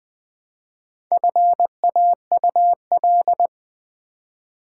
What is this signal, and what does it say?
Morse code, heard as a keyed tone, about 20 wpm, 700 Hz. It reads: FAUL
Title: Build A Word - 1 Letter At A Time - 3 to 5-Letter Words - 20wpm